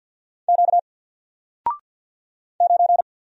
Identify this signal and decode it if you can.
Morse code, heard as a keyed tone, about 48 wpm, 700 Hz. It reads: Q 9
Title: Single Letter-Number 50wpm